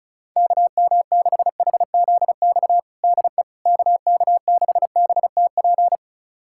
Morse code, heard as a keyed tone, about 34 wpm, 700 Hz. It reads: KM6HZX DE KK6BTP
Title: USA Callsign DE Callsign 35wpm